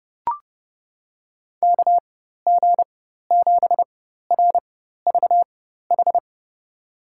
Morse code, heard as a keyed tone, about 30 wpm, spaced as for 14 wpm, 700 Hz. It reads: KG7RVH
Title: USA Call Signs - 6 Characters - 4x Character Spacing - 30wpm